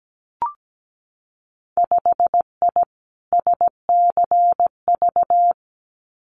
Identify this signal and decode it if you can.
Morse code, heard as a keyed tone, about 17 wpm, 700 Hz. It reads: 5I SCV
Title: Winter Field Day Exchanges Encore 17wpm